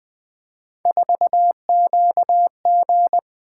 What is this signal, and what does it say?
Morse code, heard as a keyed tone, about 20 wpm, 700 Hz. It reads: 4QG
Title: Three Letter-Number Combinations - 20wpm